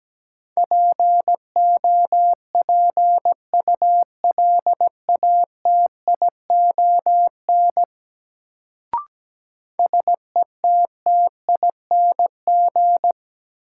Morse code, heard as a keyed tone, about 17 wpm, 700 Hz. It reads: POPULATION SETTING
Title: Sentences from Top 1300 Words - New Word Drill Alt - 17wpm